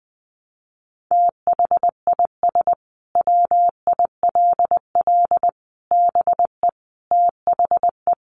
Morse code, heard as a keyed tone, about 19 wpm, 700 Hz. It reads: THIS WILL BE THE
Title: Sentences from Top 1200 Words 20wpm